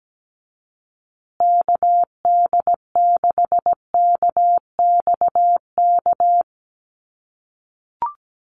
Morse code, heard as a keyed tone, about 17 wpm, 700 Hz. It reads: KD6KXK
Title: USA Call Signs Repeated 2x 17wpm